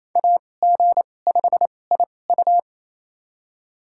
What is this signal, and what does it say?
Morse code, heard as a keyed tone, about 28 wpm, spaced as for 20 wpm, 700 Hz. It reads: AG5IU
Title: USA Call Signs - 5 Characters - 2x Character Spacing - 28wpm